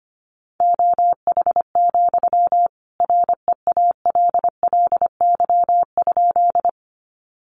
Morse code, heard as a keyed tone, about 25 wpm, 700 Hz. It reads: OH, REALLY?
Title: Sentences from Top 600 Words 25wpm